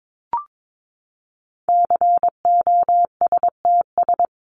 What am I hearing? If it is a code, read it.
Morse code, heard as a keyed tone, about 22 wpm, 700 Hz. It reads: COSTS
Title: ES:S Suffix - Top 100 Words with Base Words 22wpm